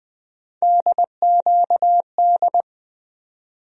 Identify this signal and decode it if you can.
Morse code, heard as a keyed tone, about 20 wpm, 700 Hz. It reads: DQD
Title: USA Call Sign Postfixes - 3 Letters 20wpm